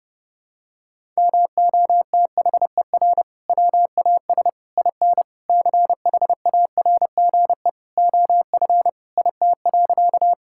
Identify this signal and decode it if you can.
Morse code, heard as a keyed tone, about 32 wpm, 700 Hz. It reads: MOTHER WAS IN CHARGE OF IT.
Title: Sentences from Top 1000 Words - Review Part 6 - 30wpm